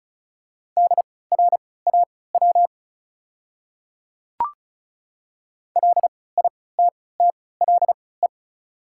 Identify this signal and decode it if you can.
Morse code, heard as a keyed tone, about 34 wpm, spaced as for 20 wpm, 690 Hz. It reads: DRAW LITTLE
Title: 3x Character Spacing - Top 500 Words - 35wpm